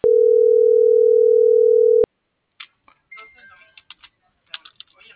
Ringback tone (North America), 440 + 480 Hz.